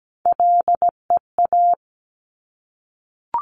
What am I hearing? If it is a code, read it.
Morse code, heard as a keyed tone, about 17 wpm, 700 Hz. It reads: LEA
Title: Top Three Letter Combinations in Words 17wpm